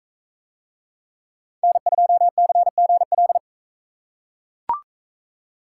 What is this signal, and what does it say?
Morse code, heard as a keyed tone, about 38 wpm, 700 Hz. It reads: N2CGL